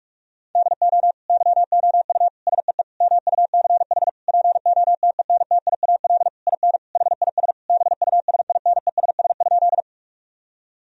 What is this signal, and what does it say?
Morse code, heard as a keyed tone, about 41 wpm, 700 Hz. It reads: DO YOU SEE MUCH POTENTIAL IN HIS BUSINESS?
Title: Sentences from Top 1000 Words - Review Part 4 - 45wpm